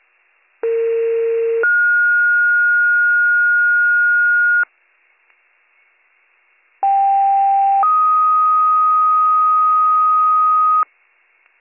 Two-tone page, heard as a one-second tone, then a three-second tone, about 460 then 1430 Hz, about 790 then 1250 Hz. It is heard from 0.6 to 10.8 s.